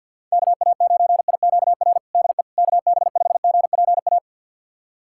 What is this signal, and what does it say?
Morse code, heard as a keyed tone, about 47 wpm, 700 Hz. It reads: KA9IQR DE KB5GPA
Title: USA Callsign DE Callsign - 6 Characters - 50wpm